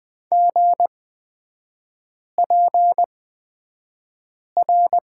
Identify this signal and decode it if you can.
Morse code, heard as a keyed tone, about 20 wpm, spaced as for 5 wpm, 690 Hz. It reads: GPR
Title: 236 Lesson 38e - Letter q - Three Characters 20wpm